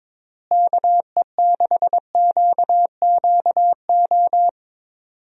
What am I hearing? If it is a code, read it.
Morse code, heard as a keyed tone, about 22 wpm, 700 Hz. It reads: KE6QQO
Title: USA Call Signs - 6 Characters - Repeated 3x - 22wpm